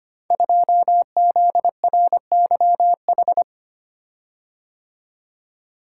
Morse code, heard as a keyed tone, about 25 wpm, 700 Hz. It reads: 2ZRYH
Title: Code Groups - 5 Letter:Numbers - 25wpm